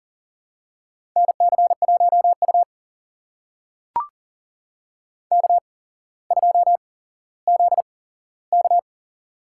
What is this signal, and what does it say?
Morse code, heard as a keyed tone, about 38 wpm, spaced as for 17 wpm, 700 Hz. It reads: NC1U K2ZK
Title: USA Call Signs - 4 Characters - 8x Character Spacing - 40wpm